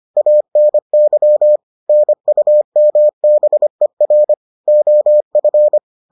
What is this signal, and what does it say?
Morse code, heard as a keyed tone, about 25 wpm, 600 Hz. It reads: ANY NUMBER OF